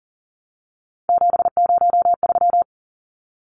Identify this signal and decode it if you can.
Morse code, heard as a keyed tone, about 40 wpm, 700 Hz. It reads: 703
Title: Numbers - 3-Digit 40wpm